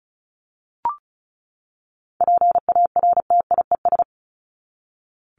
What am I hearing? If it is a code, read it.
Morse code, heard as a keyed tone, about 34 wpm, 700 Hz. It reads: PARTIES
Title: ES:S Suffix - Top 100 Words with Base Words 35wpm